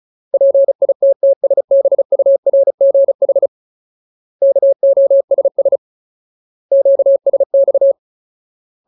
Morse code, heard as a keyed tone, about 34 wpm, 550 Hz. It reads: PITTSBURGH KOSS QSX
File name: Ultimate Word List / 35WPM Ultimate